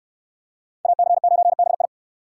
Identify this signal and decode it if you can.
Morse code, heard as a keyed tone, about 68 wpm, 690 Hz. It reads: AC9BI